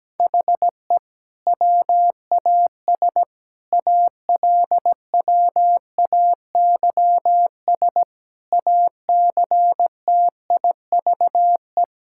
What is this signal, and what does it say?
Morse code, heard as a keyed tone, about 17 wpm, 700 Hz. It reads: HE WAS ALWAYS ACTIVE
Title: Sentences from Top 1000 Words - Review Part 6 - 17wpm